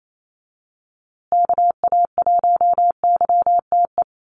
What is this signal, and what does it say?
Morse code, heard as a keyed tone, about 28 wpm, 700 Hz. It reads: KA1YTE
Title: USA Call Signs Repeated 3x 28wpm